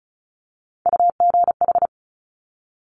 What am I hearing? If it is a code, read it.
Morse code, heard as a keyed tone, about 36 wpm, 700 Hz. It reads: UGH
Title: Top Three Letter Combinations in Words 35wpm